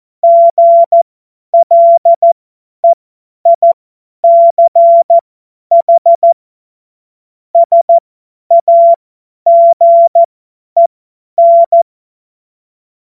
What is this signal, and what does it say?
Morse code, heard as a keyed tone, about 14 wpm, spaced as for 10 wpm, 680 Hz. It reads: GLEICH SAGEN
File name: Morsekurs / Witze / Witz 20. Januar 2026 50 F70